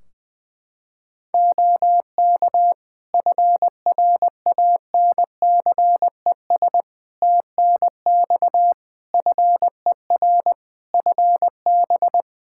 Morse code, heard as a keyed tone, about 20 wpm, 700 Hz. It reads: OK FRANCES TNX FER FB
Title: Practice QSO - Exchange 4 - Ending - 20wpm